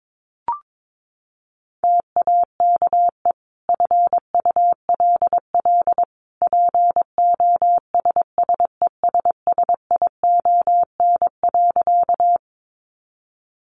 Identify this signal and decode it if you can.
Morse code, heard as a keyed tone, about 22 wpm, 700 Hz. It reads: TAKE FULL POSSESSION.
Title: Sentences from Top 1400 Words 22wpm